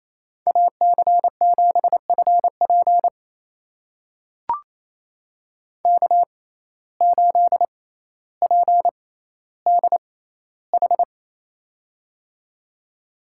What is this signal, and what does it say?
Morse code, heard as a keyed tone, about 28 wpm, spaced as for 14 wpm, 700 Hz. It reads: AC7FP K8PDH
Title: USA Call Signs - 5 Characters - 6x Character Spacing - 28wpm